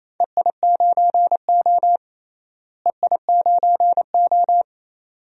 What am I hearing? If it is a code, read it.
Morse code, heard as a keyed tone, about 28 wpm, 700 Hz. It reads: EI9O EI9O
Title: DXCC-100 Call Signs - 4 Characters - Repeated 2x - 28wpm